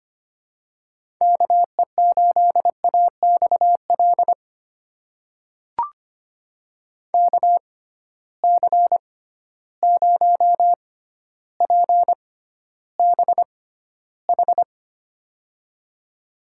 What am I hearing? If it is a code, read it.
Morse code, heard as a keyed tone, about 25 wpm, spaced as for 13 wpm, 700 Hz. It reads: KE8AXL KC0PBH
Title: USA Call Signs - 6 Characters - 6x Character Spacing - 25wpm